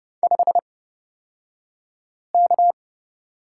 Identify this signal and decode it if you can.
Morse code, heard as a keyed tone, about 30 wpm, spaced as for 5 wpm, 700 Hz. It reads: 5K
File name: Lesson 38d - Letter q - Two Characters 30wpm